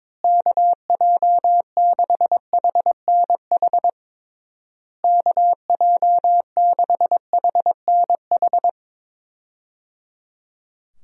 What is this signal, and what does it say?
Morse code, heard as a keyed tone, about 22 wpm, 700 Hz. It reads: KJ6HNH KJ6HNH
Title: USA Call Signs - 6 Characters - Repeated 3x - 22wpm